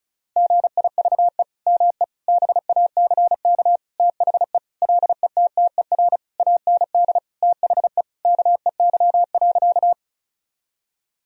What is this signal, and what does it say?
Morse code, heard as a keyed tone, about 36 wpm, 700 Hz. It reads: GIVE ME BACK THE LETTER AND THE KEY.
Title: Sentences from Top 1000 Words - Review Part 4 - 35wpm